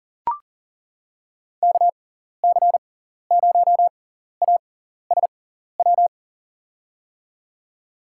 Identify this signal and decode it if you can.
Morse code, heard as a keyed tone, about 40 wpm, spaced as for 14 wpm, 700 Hz. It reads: KC0ASW
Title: USA Call Signs - 6 Characters - 6x Character Spacing - 40wpm